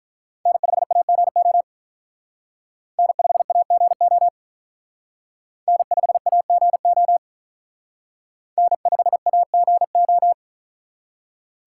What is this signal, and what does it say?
Morse code, heard as a keyed tone, about 45 wpm, 700 Hz. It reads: N5AGO N5AGO N5AGO N5AGO